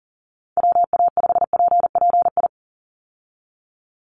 Morse code, heard as a keyed tone, about 40 wpm, 700 Hz. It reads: WA5PPI
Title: USA Call Signs 40wpm